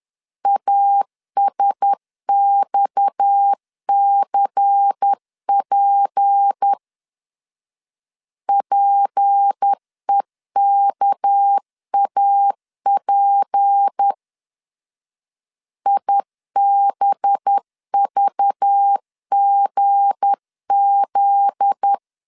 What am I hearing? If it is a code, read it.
Morse code, heard as a keyed tone, about 10 wpm, 800 Hz. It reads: ASXCP PEKAP IBVGZ
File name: LEZIONI / 7 GF9Z / 7 GF9Z 10wpm 2611